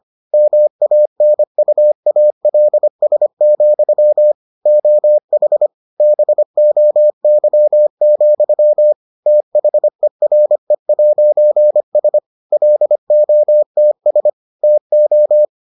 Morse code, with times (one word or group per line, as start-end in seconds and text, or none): 0.3-4.3 MANUALS,
4.7-5.7 OH
6.0-8.9 BOY,
9.3-12.2 THERE'S
12.5-14.3 LOTS
14.6-15.4 TO